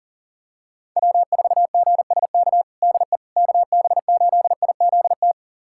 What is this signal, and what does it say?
Morse code, heard as a keyed tone, about 40 wpm, 700 Hz. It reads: W4GSK DE KB8IZT
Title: USA Callsign DE Callsign 40wpm